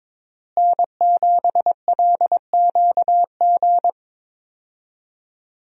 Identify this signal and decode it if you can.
Morse code, heard as a keyed tone, about 22 wpm, 700 Hz. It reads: N7LQG
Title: USA Call Signs - 5 Characters - Repeated 3x - 22wpm